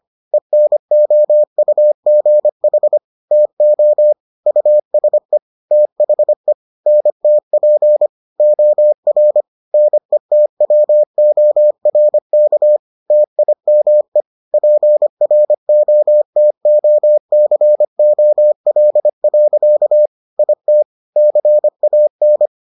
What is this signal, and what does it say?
Morse code, heard as a keyed tone, about 25 wpm, 600 Hz. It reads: ENOUGH TO USE THE NTP OR NETWORK TIME PROTOCOL. IT CAN